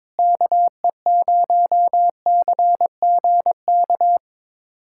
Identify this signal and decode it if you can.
Morse code, heard as a keyed tone, about 22 wpm, 700 Hz. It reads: KE0CGK
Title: USA Callsign DE Callsign 22wpm